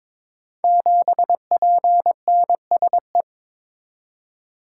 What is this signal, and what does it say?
Morse code, heard as a keyed tone, about 22 wpm, 690 Hz. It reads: 7PNSE
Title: Code Groups - 5 Letter:Numbers - 22wpm